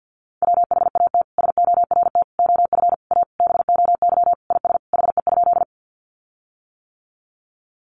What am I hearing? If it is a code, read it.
Morse code, heard as a keyed tone, about 50 wpm, 700 Hz. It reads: WHAT SORT OF A BOY IS HE?
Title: Sentences from Top 1300 Words 50wpm